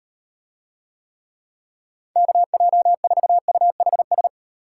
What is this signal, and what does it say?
Morse code, heard as a keyed tone, about 38 wpm, 700 Hz. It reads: KJ4UHS